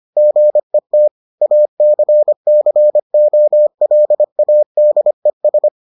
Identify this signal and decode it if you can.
Morse code, heard as a keyed tone, about 25 wpm, 600 Hz. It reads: GET ACCOLADES